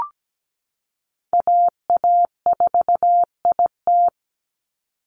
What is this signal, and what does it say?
Morse code, heard as a keyed tone, about 17 wpm, 700 Hz. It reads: AA4IT
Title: NA SOTA Chaser Callsigns 17wpm